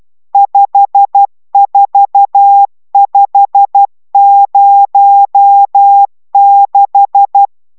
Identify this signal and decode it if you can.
Morse code, heard as a keyed tone, about 12 wpm, 800 Hz. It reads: 54506